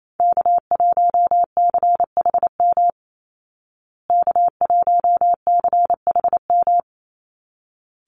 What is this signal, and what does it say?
Morse code, heard as a keyed tone, about 28 wpm, 700 Hz. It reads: K1CHM K1CHM
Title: Call Signs Repeated 3x - 28wpm